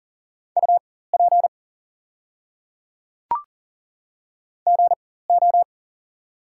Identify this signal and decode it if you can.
Morse code, heard as a keyed tone, about 40 wpm, spaced as for 19 wpm, 700 Hz. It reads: UP GO
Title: 4x Character Spacing - Top 200 Words - 40wpm